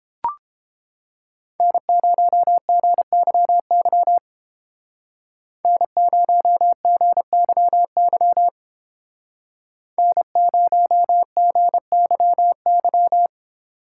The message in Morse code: N0GYY N0GYY N0GYY

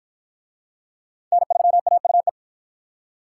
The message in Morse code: N3AUE

